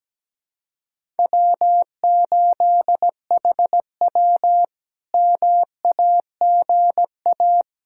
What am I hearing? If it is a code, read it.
Morse code, heard as a keyed tone, about 17 wpm, 700 Hz. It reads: W8HW MAGA
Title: K1USN SST QSO Practice - 17wpm